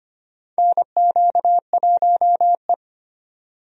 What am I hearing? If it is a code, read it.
Morse code, heard as a keyed tone, about 25 wpm, 700 Hz. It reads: NQ1E
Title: USA Call Signs - 4 Characters - 25wpm